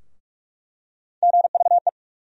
Morse code, heard as a keyed tone, about 43 wpm, 700 Hz. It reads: GVE
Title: USA Call Sign Postfixes - 3 Letters 45wpm